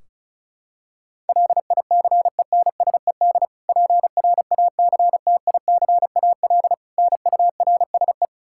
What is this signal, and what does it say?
Morse code, heard as a keyed tone, about 36 wpm, 700 Hz. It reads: LICENSED PRACTICAL NURSE
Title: Practice QSO - Exchange 3 - More Info - 35wpm